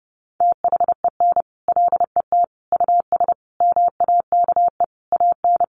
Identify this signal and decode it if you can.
Morse code, heard as a keyed tone, about 30 wpm, 700 Hz. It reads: THEN LET US MAKE AN
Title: Sentences from Top 1200 Words 30wpm